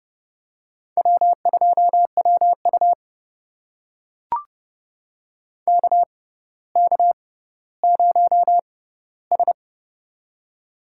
Morse code, heard as a keyed tone, about 30 wpm, spaced as for 15 wpm, 700 Hz. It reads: W2WU KK0S
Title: USA Call Signs - 4 Characters - 6x Character Spacing - 30wpm